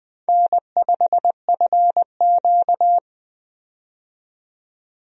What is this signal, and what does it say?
Morse code, heard as a keyed tone, about 20 wpm, 700 Hz. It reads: N5FQ
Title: USA Call Signs - 4 Characters - 20wpm